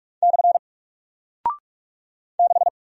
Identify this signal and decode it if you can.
Morse code, heard as a keyed tone, about 43 wpm, 700 Hz. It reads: / 6
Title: Single Character (Letter:Number:Punctuation:Prosign) Alt 45wpm